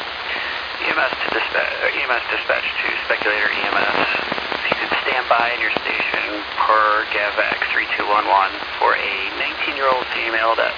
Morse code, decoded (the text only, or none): none